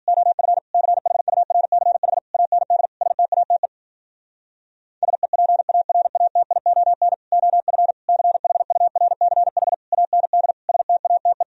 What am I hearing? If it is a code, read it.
Morse code, heard as a keyed tone, about 60 wpm, 700 Hz. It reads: OF CHURCH AND STATE SEPARATION OF CHURCH AND STATE